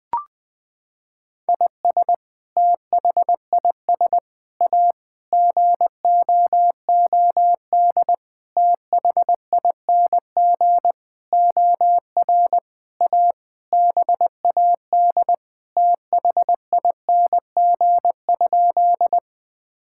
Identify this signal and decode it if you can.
Morse code, heard as a keyed tone, about 20 wpm, 700 Hz. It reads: IS THIS A GOOD THING OR A BAD THING?
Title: Sentences with 7-Word Phrases - 20wpm